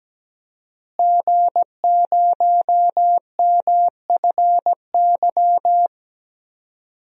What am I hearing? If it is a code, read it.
Morse code, heard as a keyed tone, about 17 wpm, 700 Hz. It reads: G0MFY
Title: DXCC-100 Call Signs - 5 Characters - Repeated 3x - 17wpm